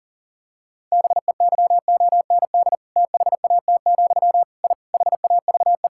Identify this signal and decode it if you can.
Morse code, heard as a keyed tone, about 40 wpm, 700 Hz. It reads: BEYOND THAT, I HAVE
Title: Sentences from Top 1000 Words - Review Part 6 - 40wpm